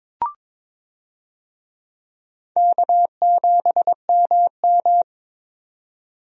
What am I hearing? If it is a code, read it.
Morse code, heard as a keyed tone, about 22 wpm, 700 Hz. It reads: K7MM
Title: USA Call Signs - 4 Characters - 22wpm